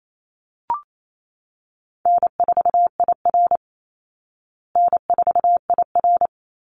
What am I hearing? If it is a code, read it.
Morse code, heard as a keyed tone, about 28 wpm, 700 Hz. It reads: N4IR N4IR
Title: Easy Call Signs Repeated 2x - 28wpm